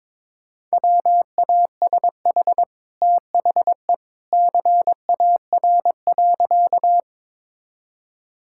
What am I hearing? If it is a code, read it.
Morse code, heard as a keyed tone, about 22 wpm, 700 Hz. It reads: WASH THE CAR.
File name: practice / 22wpm / Easy Sentences with Hints - 22wpm